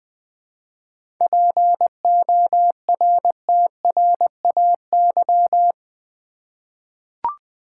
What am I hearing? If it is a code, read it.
Morse code, heard as a keyed tone, about 20 wpm, 690 Hz. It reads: PORTRAY
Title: 7 Letter Words - First 3 Letters Given - 20wpm